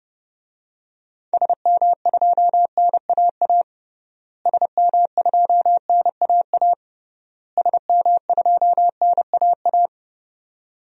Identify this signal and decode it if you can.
Morse code, heard as a keyed tone, about 30 wpm, 700 Hz. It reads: SM2NAA SM2NAA SM2NAA
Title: DXCC-100 Call Signs - 6 Characters - Repeated 3x - 30wpm